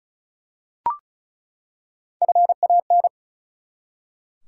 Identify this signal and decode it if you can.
Morse code, heard as a keyed tone, about 36 wpm, 700 Hz. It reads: FAN
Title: Three Letter Words - 35wpm